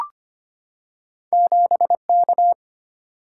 Morse code, heard as a keyed tone, about 25 wpm, 700 Hz. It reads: 7K
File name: Two Letter-Number Combinations - 25wpm